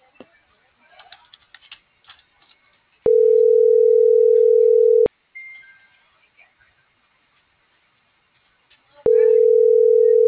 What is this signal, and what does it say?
Ringback tone (North America), 440 + 480 Hz.